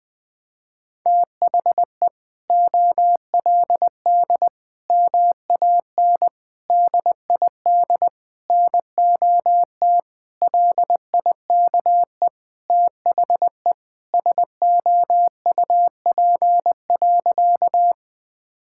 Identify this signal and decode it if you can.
Morse code, heard as a keyed tone, about 20 wpm, 700 Hz. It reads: THE OLD MAN DID NOT LIKE THE SOUP.
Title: Sentences from Top 1000 Words - Review Part 6 - 20wpm